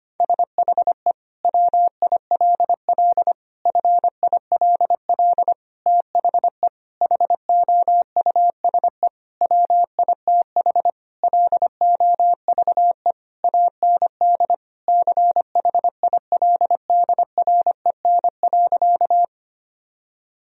Morse code, heard as a keyed tone, about 25 wpm, 700 Hz. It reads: SHE WILL FILL THE HOUSE WITH LOVE AND CHILDREN.
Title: Sentences from Top 1000 Words - Review Part 3 - 25wpm